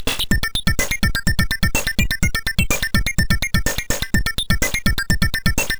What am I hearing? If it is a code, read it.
There is no code.